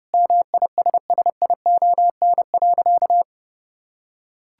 Morse code, heard as a keyed tone, about 30 wpm, 700 Hz. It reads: MISSION.
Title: Sentences from Top 1600 Words 30wpm